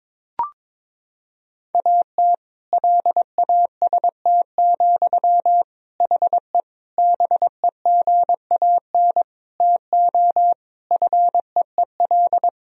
Morse code, heard as a keyed tone, about 21 wpm, 700 Hz. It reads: AT LAST, HE BEGAN TO FEEL
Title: Sentences from Top 1000 Words - Review Part 1 - 22wpm